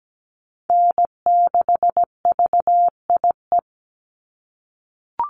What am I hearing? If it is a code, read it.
Morse code, heard as a keyed tone, about 17 wpm, 700 Hz. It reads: N6VIE
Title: USA Call Signs 17wpm